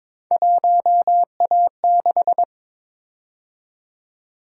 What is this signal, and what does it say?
Morse code, heard as a keyed tone, about 22 wpm, 700 Hz. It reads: 1A6
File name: practice / 22wpm / Three Letter-Number Combinations - 22wpm